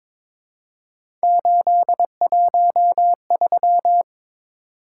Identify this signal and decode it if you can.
Morse code, heard as a keyed tone, about 22 wpm, 700 Hz. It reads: 813
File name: practice / 22wpm / Three Letter-Number Combinations - 22wpm